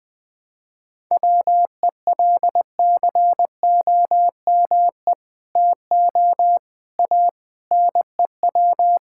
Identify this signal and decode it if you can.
Morse code, heard as a keyed tone, about 20 wpm, 700 Hz. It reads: WELCOME TO A NEW
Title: Sentences with 2-Word Phrases - 20wpm